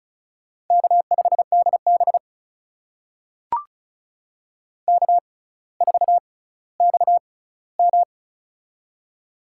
Morse code, heard as a keyed tone, about 35 wpm, spaced as for 18 wpm, 700 Hz. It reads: K5DB K4XM
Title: USA Call Signs - 4 Characters - 6x Character Spacing - 35wpm